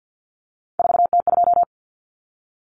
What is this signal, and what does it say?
Morse code, heard as a keyed tone, about 48 wpm, 700 Hz. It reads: 4T2
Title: Common Cut Numbers - 3-Digit 50wpm